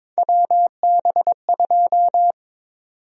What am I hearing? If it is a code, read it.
Morse code, heard as a keyed tone, about 22 wpm, 700 Hz. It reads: WB2
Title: USA Call Sign Prefixes - 22wpm